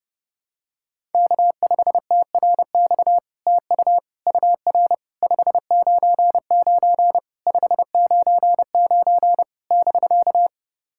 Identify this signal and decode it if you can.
Morse code, heard as a keyed tone, about 32 wpm, 700 Hz. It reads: K5TRX TU UR 599 599 <BK>
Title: SOTA QSO Practice 30wpm